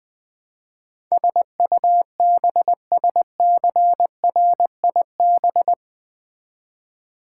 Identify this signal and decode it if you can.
Morse code, heard as a keyed tone, about 20 wpm, 700 Hz. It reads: SUBSCRIB
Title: Build A Word - 4 Letters At A Time - 10-Letter Words - 20wpm